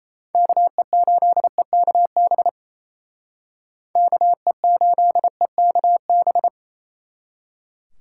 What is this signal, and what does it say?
Morse code, heard as a keyed tone, about 33 wpm, 700 Hz. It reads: KE8EKB KE8EKB
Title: Speed Racing - USA 2x3 Call Signs 28wpm